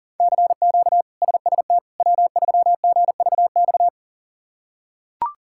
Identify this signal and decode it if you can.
Morse code, heard as a keyed tone, about 40 wpm, 700 Hz. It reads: CQ SST W3GVX
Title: K1USN SST QSO Practice - 40wpm